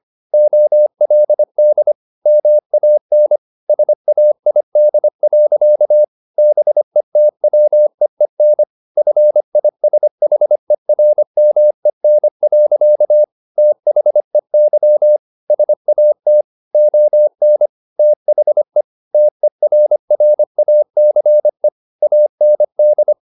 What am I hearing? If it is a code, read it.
Morse code, heard as a keyed tone, about 25 wpm, 600 Hz. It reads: OLD MAN SAID. BETWEEN FISHERMEN. THEY SAT ON THE TERRACE AND
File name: the old man and the sea part 1 25wpm